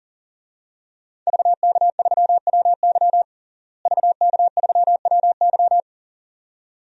Morse code, heard as a keyed tone, about 40 wpm, 700 Hz. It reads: VK3WY VK3WY
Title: DXCC-100 Call Signs - 5 Characters - Repeated 2x - 40wpm